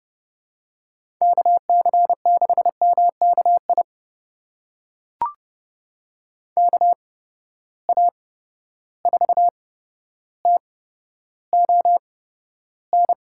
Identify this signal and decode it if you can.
Morse code, heard as a keyed tone, about 32 wpm, spaced as for 13 wpm, 700 Hz. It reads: KC6MKI KA4TON
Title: USA Call Signs - 6 Characters - 8x Character Spacing - 30wpm